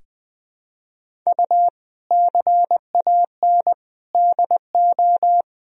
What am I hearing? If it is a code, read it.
Morse code, heard as a keyed tone, about 20 wpm, 700 Hz. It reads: U CAN DO
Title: Sentences from Top 100 Words and Abbreviations 20wpm